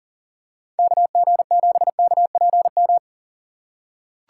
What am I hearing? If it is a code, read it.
Morse code, heard as a keyed tone, about 40 wpm, 700 Hz. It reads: KG7KPM
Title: USA Call Signs - 6 Characters - 6x Character Spacing - 40wpm